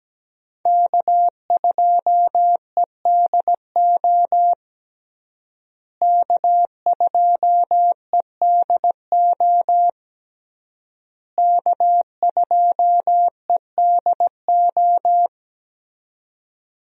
Morse code, heard as a keyed tone, about 17 wpm, 700 Hz. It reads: K2EDO K2EDO K2EDO
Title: USA Call Signs - 5 Characters - Repeated 3x - 17wpm